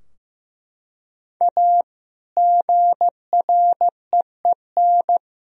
Morse code, heard as a keyed tone, about 15 wpm, 700 Hz. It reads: A GREEN
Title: Sentences from Top 1000 Words - Review Part 2 - 15wpm